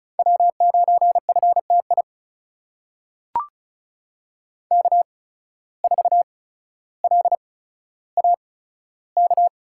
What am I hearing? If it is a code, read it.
Morse code, heard as a keyed tone, about 36 wpm, spaced as for 15 wpm, 700 Hz. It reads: W9FTI K4LAK
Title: USA Call Signs - 5 Characters - 8x Character Spacing - 35wpm